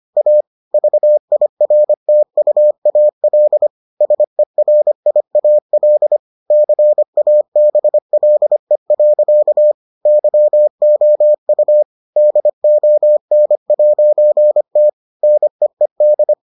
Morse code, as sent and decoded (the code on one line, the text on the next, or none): .- / ...- .. .-. - ..- .- .-.. / ... .-. .. .- .-.. / -.-. .- -... .-.. .-.-.- / -.-- --- ..- / -.. --- -. .----. - / -. . -..
A VIRTUAL SERIAL CABLE. YOU DON'T NEED